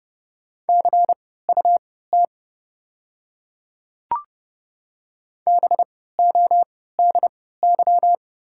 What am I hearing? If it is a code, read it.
Morse code, heard as a keyed tone, about 30 wpm, spaced as for 17 wpm, 700 Hz. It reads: CUT BODY
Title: 3x Character Spacing - Top 500 Words - 30wpm